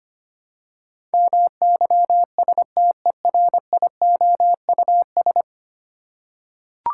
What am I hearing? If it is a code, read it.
Morse code, heard as a keyed tone, about 25 wpm, 700 Hz. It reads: MYSTERIOUS
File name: 10 Letter Words - First 3 Letters Given - 25wpm